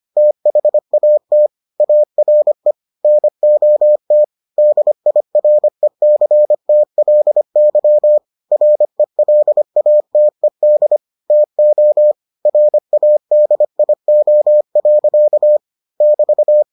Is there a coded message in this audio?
Yes: Morse code from 0.2 to 16.6 s, heard as a keyed tone, about 25 wpm, 600 Hz. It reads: THAT ARE NOT DIRECTLY RELATED TO RADIO. <BT>